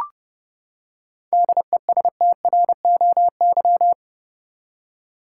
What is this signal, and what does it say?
Morse code, heard as a keyed tone, about 31 wpm, 700 Hz. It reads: DESTROY
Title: Seven Letter Words - 30wpm